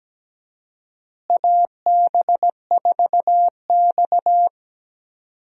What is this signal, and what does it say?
Morse code, heard as a keyed tone, about 17 wpm, 700 Hz. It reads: AB4X
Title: USA Call Signs - 4 Characters - Repeated 3x - 17wpm